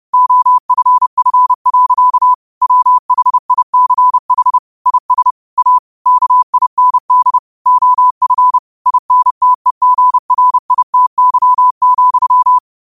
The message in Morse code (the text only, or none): OFF. WHICH IS A KIND OF INTEGRITY,